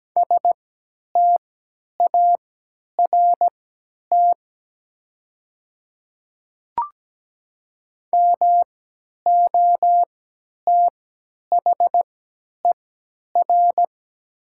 Morse code, heard as a keyed tone, about 17 wpm, spaced as for 10 wpm, 700 Hz. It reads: START MOTHER
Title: 3x Character Spacing - Top 300 Words - 17wpm